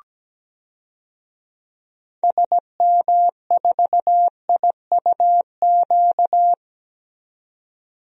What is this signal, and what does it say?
Morse code, heard as a keyed tone, about 17 wpm, 700 Hz. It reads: SM4IUQ
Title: DXCC-100 Call Signs - 6 Characters - 17wpm